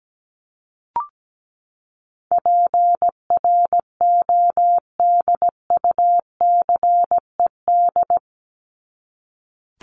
Morse code, heard as a keyed tone, about 17 wpm, 700 Hz. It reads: PRODUCED
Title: ED Suffix - Top 100 Words with Base Words 17wpm